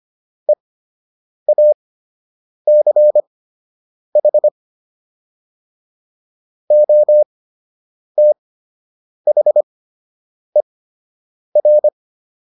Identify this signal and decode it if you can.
Morse code, heard as a keyed tone, about 25 wpm, spaced as for 8 wpm, 600 Hz. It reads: EACH OTHER